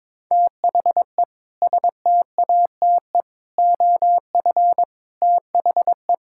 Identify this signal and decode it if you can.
Morse code, heard as a keyed tone, about 22 wpm, 700 Hz. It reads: THE STATE OF THE
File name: practice / 22wpm / Sentences with 3-Word Phrases - 22wpm